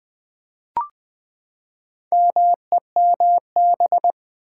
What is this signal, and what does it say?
Morse code, heard as a keyed tone, about 20 wpm, 700 Hz. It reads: MEMB
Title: Build A Word - 4 Letters At A Time - 10-Letter Words - 20wpm